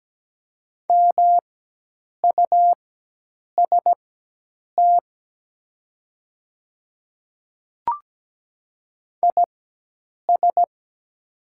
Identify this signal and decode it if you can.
Morse code, heard as a keyed tone, about 17 wpm, spaced as for 8 wpm, 700 Hz. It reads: MUST IS
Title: 4x Character Spacing - Top 300 Words - 17wpm